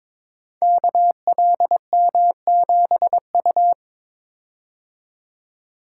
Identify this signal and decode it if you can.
Morse code, heard as a keyed tone, about 22 wpm, 700 Hz. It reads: KLM7U
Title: Code Groups - 5 Letter:Numbers - 22wpm